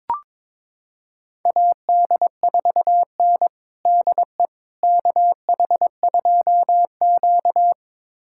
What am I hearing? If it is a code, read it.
Morse code, heard as a keyed tone, about 22 wpm, 700 Hz. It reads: AD4N DE KH2Q
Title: USA Callsign DE Callsign - 4 Characters - 22wpm